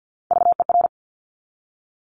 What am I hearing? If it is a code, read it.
Morse code, heard as a keyed tone, about 50 wpm, 700 Hz. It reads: VER
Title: Top Three Letter Combinations in Words 50wpm